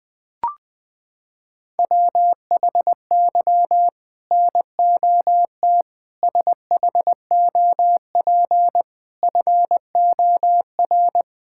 Morse code, heard as a keyed tone, about 20 wpm, 700 Hz. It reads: WHY NOT SHOP FOR
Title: Sentences from Top 1600 Words 20wpm